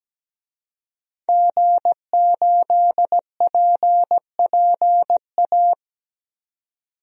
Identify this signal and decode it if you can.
Morse code, heard as a keyed tone, about 17 wpm, 700 Hz. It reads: G8PPA
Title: DXCC-100 Call Signs - 5 Characters - 17wpm